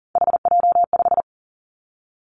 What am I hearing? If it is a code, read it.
Morse code, heard as a keyed tone, about 40 wpm, 700 Hz. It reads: HJ5
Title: Callsign Prefixes for Top 100 DXCC - 40wpm